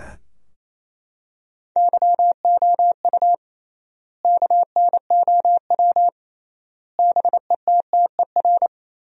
Morse code, heard as a keyed tone, about 28 wpm, 700 Hz. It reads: YOU KNOW BETTER
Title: 3x Word-Spacing - Sentences from Top 400 Words - 28wpm